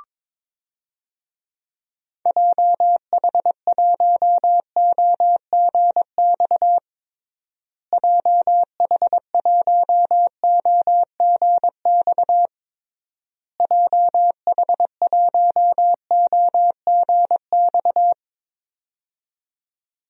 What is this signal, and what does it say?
Morse code, heard as a keyed tone, about 22 wpm, 700 Hz. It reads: JH1OGX JH1OGX JH1OGX
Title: DXCC-100 Call Signs - 6 Characters - Repeated 3x - 22wpm